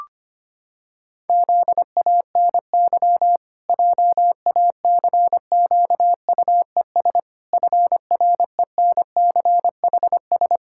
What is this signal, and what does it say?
Morse code, heard as a keyed tone, about 25 wpm, 700 Hz. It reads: ZANY JACQUES FRENCHS
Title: Pangram Challenge v2 - 25wpm